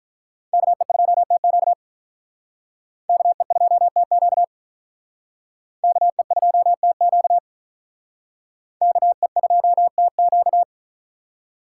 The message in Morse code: KE2TQ KE2TQ KE2TQ KE2TQ

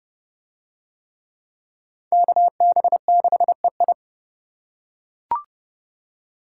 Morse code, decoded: KB6EI

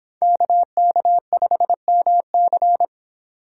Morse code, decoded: KK5MC

